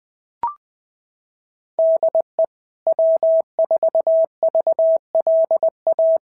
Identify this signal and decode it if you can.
Morse code, heard as a keyed tone, about 19 wpm, 650 Hz. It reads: DE W4VLA
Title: SOTA QSO Practice 20wpm